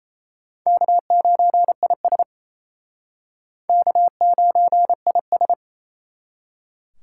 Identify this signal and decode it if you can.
Morse code, heard as a keyed tone, about 35 wpm, 700 Hz. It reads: K9IS K9IS
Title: Speed Racing - USA 1x2 Call Signs 28wpm